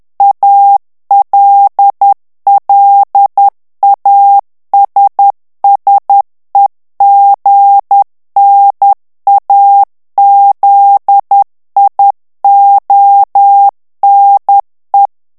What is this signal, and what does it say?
Morse code, heard as a keyed tone, about 10 wpm, 800 Hz. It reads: ALLASSEGNAZIONE
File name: ITALIEN texte clair 2 vitesse 10 mots minute